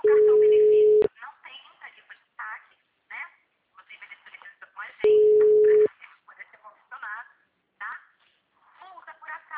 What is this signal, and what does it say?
Ringback tone (Europe, 425 Hz), with speech.
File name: out- -2051- - - .73